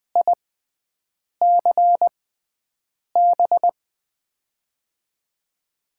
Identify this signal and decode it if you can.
Morse code, heard as a keyed tone, about 20 wpm, spaced as for 7 wpm, 700 Hz. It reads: ICB
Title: USA Call Signs - 6x Character Spacing - 20wpm